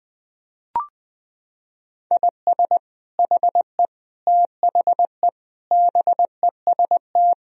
Morse code, heard as a keyed tone, about 20 wpm, 700 Hz. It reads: IS HE THE BEST
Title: Sentences from Top 1000 Words - Review Part 4 - 20wpm